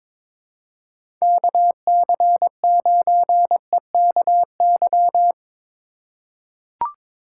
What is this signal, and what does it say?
Morse code, heard as a keyed tone, about 21 wpm, 700 Hz. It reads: KC9EKY
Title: USA Call Signs - 6 Characters - 4x Character Spacing - 22wpm